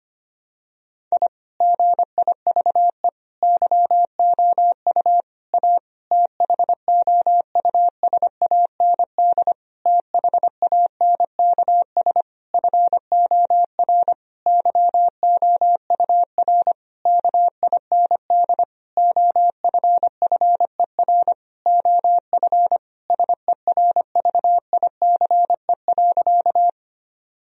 Morse code, heard as a keyed tone, about 25 wpm, 700 Hz. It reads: I GIVE YOU A THOUSAND THANKS FOR YOUR KIND OFFER OF SERVICE.
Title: Sentences from Top 1000 Words - Review Part 4 - 25wpm